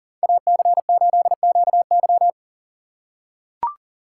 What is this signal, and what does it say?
Morse code, heard as a keyed tone, about 40 wpm, 700 Hz. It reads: AC8QY
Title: USA Call Signs - 5 Characters - 4x Character Spacing - 40wpm